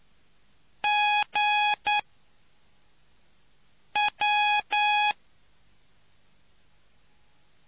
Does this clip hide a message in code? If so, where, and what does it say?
Yes: Morse code from 0.8 to 5.1 s, heard as a keyed tone, about 9 wpm, 830 Hz. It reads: G W